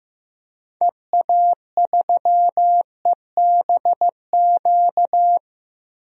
Morse code, heard as a keyed tone, about 15 wpm, 700 Hz. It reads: EA3EBQ
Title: DXCC-100 Call Signs - 6 Characters - 15wpm